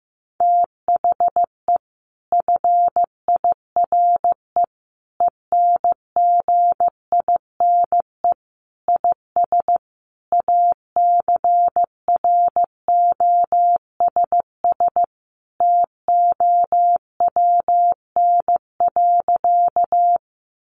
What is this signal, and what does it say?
Morse code, heard as a keyed tone, about 15 wpm, 700 Hz. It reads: THE FIRE ENGINE IS ACROSS TOWN.
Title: Sentences from Top 1500 Words 15wpm